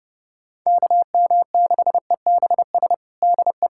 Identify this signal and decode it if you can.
Morse code, heard as a keyed tone, about 30 wpm, 700 Hz. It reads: KM6EBS DE
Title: USA Callsign DE Callsign - 6 Characters - 30wpm